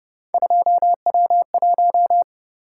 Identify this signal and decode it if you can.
Morse code, heard as a keyed tone, about 29 wpm, 700 Hz. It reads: 2W1